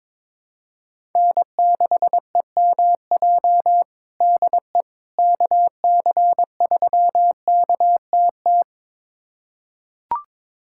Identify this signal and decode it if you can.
Morse code, heard as a keyed tone, about 21 wpm, 700 Hz. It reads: N6EMJ DE KC3KTT
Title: USA Callsign DE Callsign 22wpm